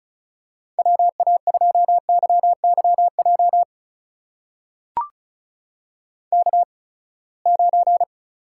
Morse code, heard as a keyed tone, about 36 wpm, spaced as for 24 wpm, 700 Hz. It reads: WA2YYJ K9
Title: USA Call Signs - 8x Character Spacing - 35wpm